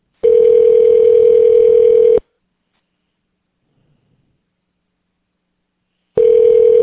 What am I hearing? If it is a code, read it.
Ringback tone (North America), 440 + 480 Hz.